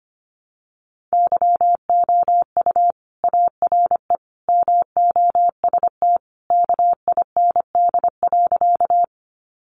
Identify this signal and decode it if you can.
Morse code, heard as a keyed tone, about 25 wpm, 700 Hz. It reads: YOU ARE MOST KIND.
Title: Sentences from Top 200 Words 25wpm